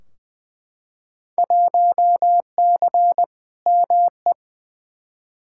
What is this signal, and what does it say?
Morse code, heard as a keyed tone, about 20 wpm, 700 Hz. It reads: 1C ME
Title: Field Day Exchanges - 20wpm